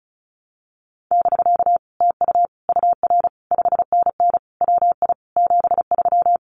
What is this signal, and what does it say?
Morse code, heard as a keyed tone, about 36 wpm, 700 Hz. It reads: <BK> TU UR 5NN WI 73
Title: Common QSO Phrases Encore 35wpm